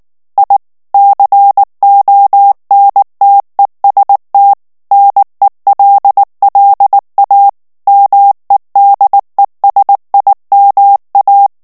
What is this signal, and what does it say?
Morse code, heard as a keyed tone, about 19 wpm, 800 Hz. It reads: I CONTEST NELLA MEDESIMA
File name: morse mp3 / ITALIEN texte clair 2 vitesse 18 mots minute